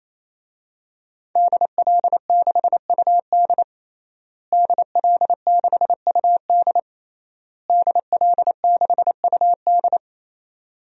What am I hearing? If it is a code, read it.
Morse code, heard as a keyed tone, about 28 wpm, 700 Hz. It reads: DL6UD DL6UD DL6UD
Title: DXCC-100 Call Signs - 5 Characters - Repeated 3x - 28wpm